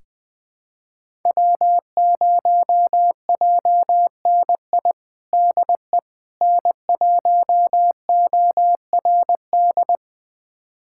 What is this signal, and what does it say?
Morse code, heard as a keyed tone, about 20 wpm, 700 Hz. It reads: W0JNI DE N1ORD
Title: USA Callsign DE Callsign - 5 Characters - 20wpm